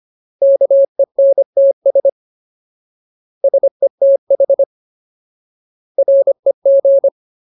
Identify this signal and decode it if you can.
Morse code, heard as a keyed tone, about 25 wpm, 550 Hz. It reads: KENTS SETH REG
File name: Ultimate Word List / 25WPM Ultimate